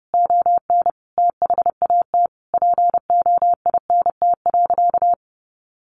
Morse code, heard as a keyed tone, about 28 wpm, 700 Hz. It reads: ON THAT POINT.
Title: Sentences from Top 1400 Words 30wpm